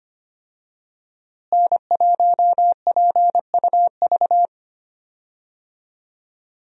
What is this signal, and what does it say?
Morse code, heard as a keyed tone, about 25 wpm, 700 Hz. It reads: N1PUV
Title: USA Call Signs - 5 Characters - Repeated 2x - 25wpm